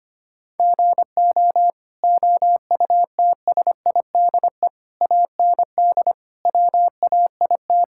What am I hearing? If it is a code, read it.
Morse code, heard as a keyed tone, about 25 wpm, 700 Hz. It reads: GO OUTSIDE AND WAIT
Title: Sentences from Top 1000 Words - Review Part 6 - 25wpm